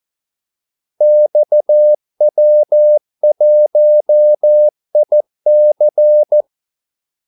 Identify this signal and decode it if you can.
Morse code, heard as a keyed tone, about 14 wpm, 600 Hz. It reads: XW1IC
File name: Callsigns-14wpm